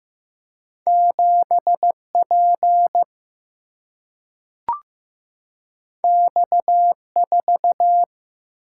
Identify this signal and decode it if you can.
Morse code, heard as a keyed tone, about 15 wpm, 700 Hz. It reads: 7 P X 4
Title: Single-Letter Single-Number 15wpm